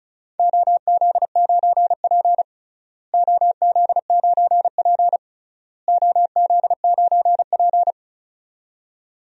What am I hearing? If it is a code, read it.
Morse code, heard as a keyed tone, about 36 wpm, 700 Hz. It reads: OZ9P OZ9P OZ9P
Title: DXCC-100 Call Signs - 4 Characters - Repeated 3x - 35wpm